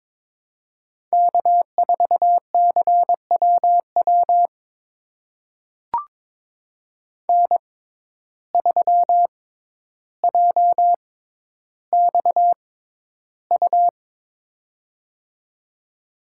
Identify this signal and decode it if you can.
Morse code, heard as a keyed tone, about 22 wpm, spaced as for 11 wpm, 700 Hz. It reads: K4CWW N3JXU
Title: USA Call Signs - 5 Characters - 6x Character Spacing - 22wpm